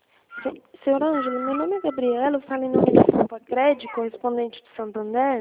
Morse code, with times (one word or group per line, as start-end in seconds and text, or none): none